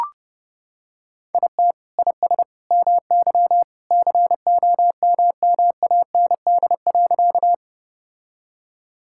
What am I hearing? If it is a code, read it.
Morse code, heard as a keyed tone, about 30 wpm, 700 Hz. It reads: IT IS MY COMMAND.
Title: Sentences from Top 1600 Words 30wpm